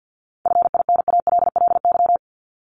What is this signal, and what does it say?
Morse code, heard as a keyed tone, about 50 wpm, 700 Hz. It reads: FINALLY
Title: Top 751-1000 Words 50wpm